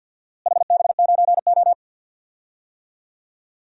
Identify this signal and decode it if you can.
Morse code, heard as a keyed tone, about 49 wpm, 700 Hz. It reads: HB9O
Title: DXCC-100 Call Signs - 4 Characters - Repeated 3x - 50wpm